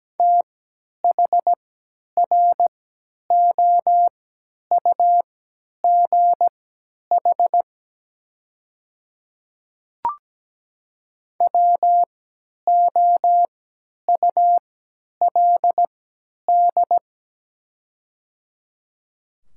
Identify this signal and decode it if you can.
Morse code, heard as a keyed tone, about 17 wpm, spaced as for 10 wpm, 700 Hz. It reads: THROUGH WOULD
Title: 3x Character Spacing - Top 200 Words - 17wpm